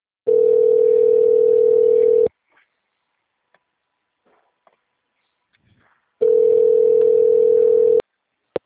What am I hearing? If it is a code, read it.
Ringback tone (North America), 440 + 480 Hz.
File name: out- -2016- - - .20